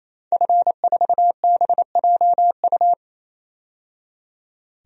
Morse code, heard as a keyed tone, about 28 wpm, 700 Hz. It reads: F4BJU
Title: DXCC-100 Call Signs - 5 Characters - 28wpm